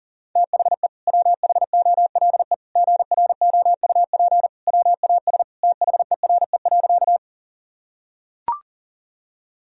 Morse code, heard as a keyed tone, about 40 wpm, 700 Hz. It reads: THE WHOLE GROUP WAS THERE.
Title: Sentences from Top 1000 Words - Review Part 1 - 40wpm